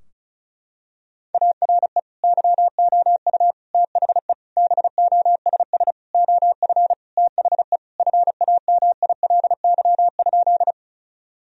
Morse code, heard as a keyed tone, about 34 wpm, 700 Hz. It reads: ARE YOU THE BOSS OF THE FAMILY?
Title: Sentences from Top 1000 Words - Review Part 4 - 35wpm